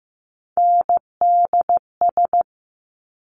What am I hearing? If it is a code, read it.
Morse code, heard as a keyed tone, about 15 wpm, 700 Hz. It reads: NDS
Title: Top Three Letter Combinations in Words 15wpm